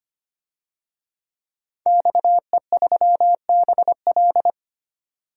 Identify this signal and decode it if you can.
Morse code, heard as a keyed tone, about 25 wpm, 700 Hz. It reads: XE3BL
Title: DXCC-100 Call Signs - 5 Characters - 25wpm